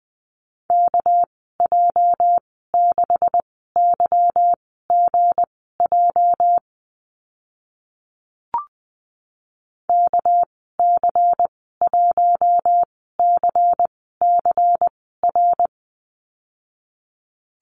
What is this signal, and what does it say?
Morse code, heard as a keyed tone, about 20 wpm, spaced as for 14 wpm, 700 Hz. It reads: KJ6YGJ KC1CCR
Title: 219 Lesson 35g - Number 0 - Call Signs 20wpm